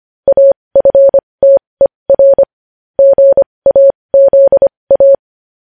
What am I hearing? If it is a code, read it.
Morse code, heard as a keyed tone, about 25 wpm, spaced as for 20 wpm, 550 Hz. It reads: AFTER GAZA
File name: podcast / News headlines-1x-25x20